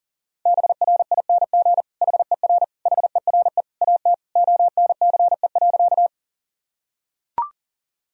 Morse code, heard as a keyed tone, about 40 wpm, 700 Hz. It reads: BRING HER HERE AT ONCE.
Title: Sentences from Top 1000 Words - Review Part 3 - 40wpm